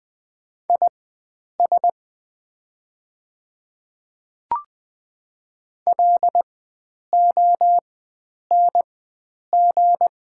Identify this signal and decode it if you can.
Morse code, heard as a keyed tone, about 21 wpm, spaced as for 9 wpm, 690 Hz. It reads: IS LONG
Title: 4x Character Spacing - Top 500 Words - 20wpm